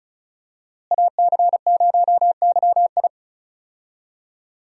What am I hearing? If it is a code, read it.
Morse code, heard as a keyed tone, about 36 wpm, 700 Hz. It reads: AC0YI
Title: USA Call Signs - 5 Characters - Repeated 2x - 35wpm